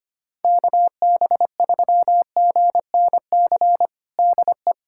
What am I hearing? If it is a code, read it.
Morse code, heard as a keyed tone, about 25 wpm, 700 Hz. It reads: KB3GNC DE
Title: USA Callsign DE Callsign - 6 Characters - 25wpm